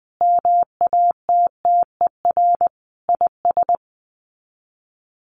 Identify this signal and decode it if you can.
Morse code, heard as a keyed tone, about 20 wpm, 700 Hz. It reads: MATTER IS
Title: Sets of 5 Words Encore2 20wpm